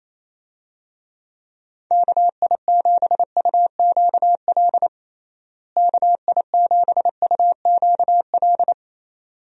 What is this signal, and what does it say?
Morse code, heard as a keyed tone, about 28 wpm, 700 Hz. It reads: KI7UQL KI7UQL
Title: USA Call Signs - 6 Characters - Repeated 2x - 28wpm